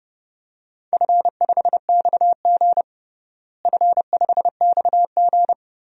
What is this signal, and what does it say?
Morse code, heard as a keyed tone, about 30 wpm, 700 Hz. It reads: F5XG F5XG
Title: DXCC-100 Call Signs - 4 Characters - Repeated 2x - 30wpm